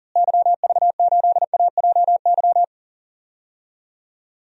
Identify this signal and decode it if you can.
Morse code, heard as a keyed tone, about 40 wpm, 700 Hz. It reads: YV8AJY
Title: DXCC-100 Call Signs - 6 Characters - Repeated 3x - 40wpm